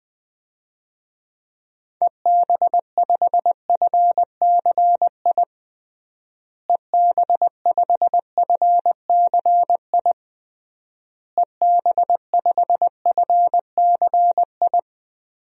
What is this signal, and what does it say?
Morse code, heard as a keyed tone, about 20 wpm, 700 Hz. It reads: EB5FCI EB5FCI EB5FCI
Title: DXCC-100 Call Signs - 6 Characters - Repeated 3x - 20wpm